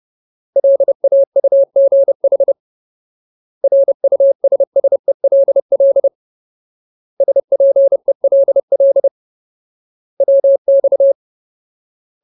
Morse code, heard as a keyed tone, about 30 wpm, 550 Hz. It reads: LAUGH RUSSELL SPELL WX
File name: Ultimate Word List / 30WPM Ultimate